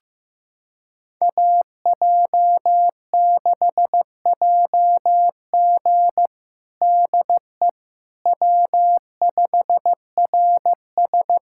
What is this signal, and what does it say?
Morse code, heard as a keyed tone, about 14 wpm, 700 Hz. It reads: AJ6JG DE W5RS
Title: USA Callsign DE Callsign 15wpm